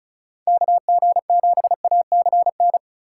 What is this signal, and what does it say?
Morse code, heard as a keyed tone, about 36 wpm, 700 Hz. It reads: KG7ACN
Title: USA Call Signs - 6 Characters - 4x Character Spacing - 35wpm